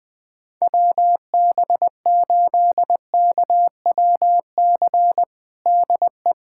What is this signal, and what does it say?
Morse code, heard as a keyed tone, about 20 wpm, 700 Hz. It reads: WB8KWC DE
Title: USA Callsign DE Callsign - 6 Characters - 20wpm